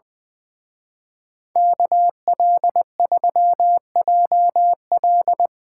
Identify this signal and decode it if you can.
Morse code, heard as a keyed tone, about 22 wpm, 700 Hz. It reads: KL3JL
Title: Speed Racing - USA 2x2 Call Signs 20wpm